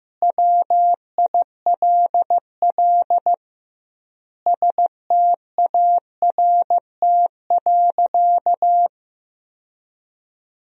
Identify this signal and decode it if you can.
Morse code, heard as a keyed tone, about 16 wpm, 700 Hz. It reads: WILL START.
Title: 2x Word-Spacing - Sentences from Top 300 Words - 15wpm